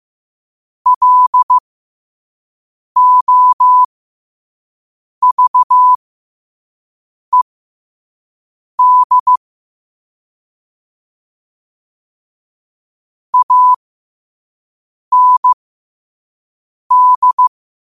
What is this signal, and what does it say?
Morse code, heard as a keyed tone, about 15 wpm, spaced as for 5 wpm, 1000 Hz. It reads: LOVED AND
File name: qotdfiles / 5wpm